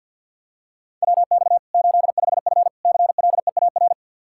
Morse code, heard as a keyed tone, about 50 wpm, 700 Hz. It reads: WX 85F CLEAR
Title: Practice QSO - Exchange 2 - Info - 50wpm